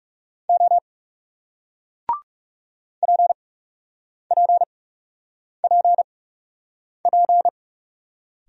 Morse code, heard as a keyed tone, about 38 wpm, 700 Hz. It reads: O P P P P